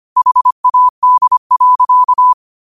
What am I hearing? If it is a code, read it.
Morse code, heard as a keyed tone, about 25 wpm, 1000 Hz. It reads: HAD.
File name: 25wpm